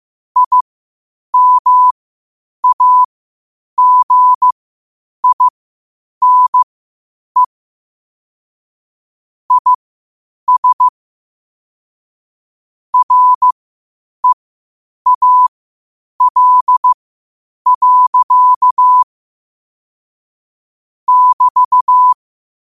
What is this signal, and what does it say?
Morse code, heard as a keyed tone, about 15 wpm, spaced as for 8 wpm, 1000 Hz. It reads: IMAGINE IS REAL. <BT>
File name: qotdfiles / 8wpm